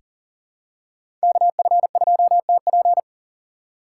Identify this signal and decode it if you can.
Morse code, heard as a keyed tone, about 40 wpm, 700 Hz. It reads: KF2TP
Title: K1USN SST QSO Practice - 40wpm